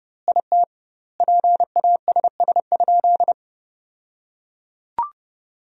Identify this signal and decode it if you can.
Morse code, heard as a keyed tone, about 30 wpm, 700 Hz. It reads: IT PASS?
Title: 2x Word-Spacing - Sentences from Top 400 Words - 30wpm